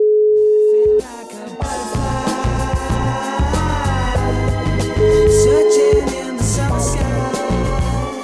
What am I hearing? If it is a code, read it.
Ringback tone (Europe, 425 Hz), with noise.